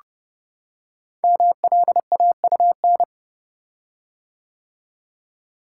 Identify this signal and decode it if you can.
Morse code, heard as a keyed tone, about 30 wpm, 700 Hz. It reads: MLAUN
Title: Code Groups - 5 Letters - 30wpm